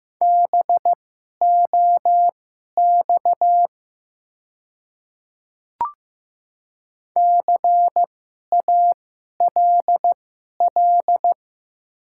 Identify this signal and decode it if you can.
Morse code, heard as a keyed tone, about 15 wpm, spaced as for 11 wpm, 700 Hz. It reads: BOX CALL
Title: 2x Character Spacing - Top 500 Words - 15wpm